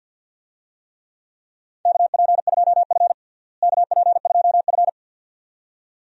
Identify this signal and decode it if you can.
Morse code, heard as a keyed tone, about 47 wpm, 700 Hz. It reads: KP2F KP2F
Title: DXCC-100 Call Signs - 4 Characters - Repeated 2x - 50wpm